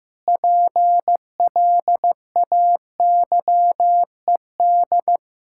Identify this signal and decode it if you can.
Morse code, heard as a keyed tone, about 16 wpm, 700 Hz. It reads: PLAYED
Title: Pangram Challenge v2 - 15wpm